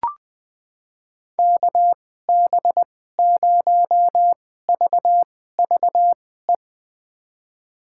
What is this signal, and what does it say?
Morse code, heard as a keyed tone, about 20 wpm, spaced as for 14 wpm, 700 Hz. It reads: KB0VVE
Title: USA Call Signs - 6 Characters - 2x Character Spacing - 20wpm